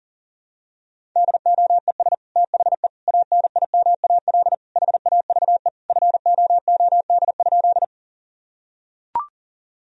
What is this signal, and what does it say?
Morse code, heard as a keyed tone, about 40 wpm, 700 Hz. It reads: DOES THE ANIMAL HAVE FOOD?
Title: Sentences from Top 1000 Words - Review Part 1 - 40wpm